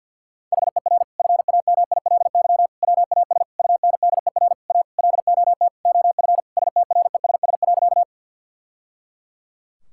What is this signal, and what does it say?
Morse code, heard as a keyed tone, about 52 wpm, 700 Hz. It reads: HER FAMILY WAS UNDER A LOT OF STRESS.
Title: Sentences from Top 1000 Words - Review Part 6 - 50wpm